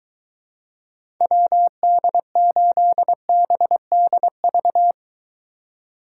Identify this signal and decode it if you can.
Morse code, heard as a keyed tone, about 23 wpm, 700 Hz. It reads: WD8BDV